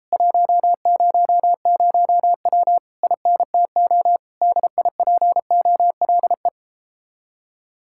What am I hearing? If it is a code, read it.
Morse code, heard as a keyed tone, about 37 wpm, 700 Hz. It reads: 100W INTO DIPOLE